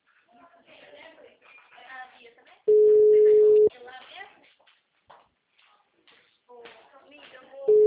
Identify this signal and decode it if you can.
Ringback tone (Europe, 425 Hz), with speech.
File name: out- -2002- - - .60295